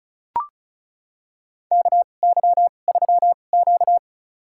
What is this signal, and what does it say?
Morse code, heard as a keyed tone, about 36 wpm, spaced as for 25 wpm, 700 Hz. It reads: KY3Q
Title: USA Call Signs - 2x Character Spacing - 35wpm